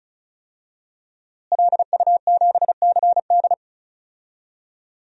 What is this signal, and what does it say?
Morse code, heard as a keyed tone, about 36 wpm, 700 Hz. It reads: LU7CD
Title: DXCC-100 Call Signs - 5 Characters - Repeated 3x - 35wpm